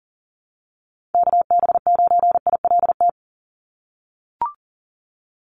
Morse code, heard as a keyed tone, about 40 wpm, 700 Hz. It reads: KB9ILT
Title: NA SOTA Activator Callsigns 40wpm